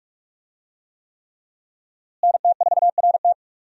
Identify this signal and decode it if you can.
Morse code, heard as a keyed tone, about 46 wpm, 700 Hz. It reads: NT4RT
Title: USA Call Signs - 5 Characters - Repeated 3x - 45wpm